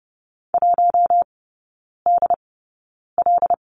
Morse code, heard as a keyed tone, about 30 wpm, spaced as for 9 wpm, 700 Hz. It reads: 1DL
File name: Lesson 18e - Letter m - Three Characters 30wpm